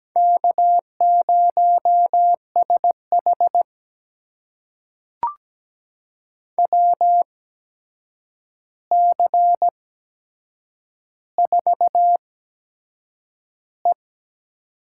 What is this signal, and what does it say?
Morse code, heard as a keyed tone, about 17 wpm, spaced as for 7 wpm, 700 Hz. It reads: K0SH WC4E
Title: USA Call Signs - 4 Characters - 8x Character Spacing - 17wpm